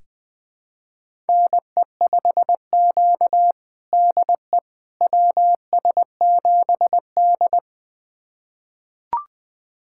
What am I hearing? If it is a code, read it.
Morse code, heard as a keyed tone, about 20 wpm, 700 Hz. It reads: NE5Q DE WS7D
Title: USA Callsign DE Callsign - 4 Characters - 20wpm